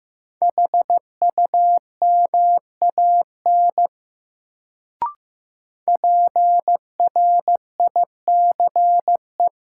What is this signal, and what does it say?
Morse code, heard as a keyed tone, about 15 wpm, 700 Hz. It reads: HUMAN PRICE
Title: Sentences from Top 600 Words - New Word Drill - 15wpm